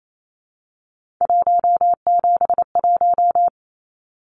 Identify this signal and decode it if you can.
Morse code, heard as a keyed tone, about 28 wpm, 700 Hz. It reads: 171
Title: Numbers - 3-Digit 28wpm